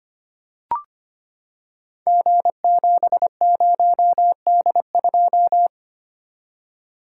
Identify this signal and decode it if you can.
Morse code, heard as a keyed tone, about 25 wpm, 700 Hz. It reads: G70D2
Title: Code Groups - 5 Letter:Numbers - 25wpm